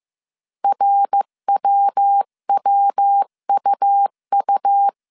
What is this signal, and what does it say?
Morse code, heard as a keyed tone, about 15 wpm, 800 Hz. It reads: RWWUU
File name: CW / DET / 1 3 1500C S C 14wpm 2458